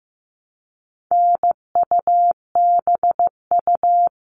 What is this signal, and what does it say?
Morse code, heard as a keyed tone, about 15 wpm, 700 Hz. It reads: NUBU
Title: All Cut Numbers - 4-Digit 15wpm